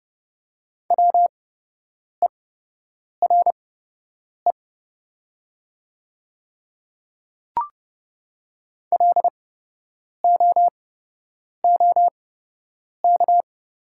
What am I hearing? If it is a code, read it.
Morse code, heard as a keyed tone, about 30 wpm, spaced as for 8 wpm, 700 Hz. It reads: WERE LOOK
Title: 8x Character Spacing - Top 100 Words - 30wpm